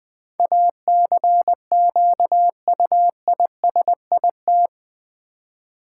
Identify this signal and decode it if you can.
Morse code, heard as a keyed tone, about 20 wpm, 700 Hz. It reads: ACQUISIT
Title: Build A Word - 4 Letters At A Time - 11-Letter Words - 20wpm